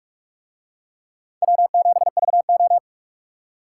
Morse code, heard as a keyed tone, about 44 wpm, 700 Hz. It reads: W7VO